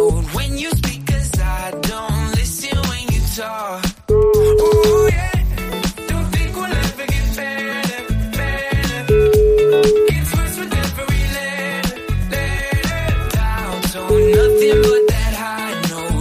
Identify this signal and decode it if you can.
Ringback tone (Europe, 425 Hz), with noise.